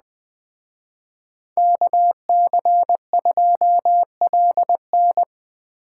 Morse code, heard as a keyed tone, about 23 wpm, 700 Hz. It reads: KC2LN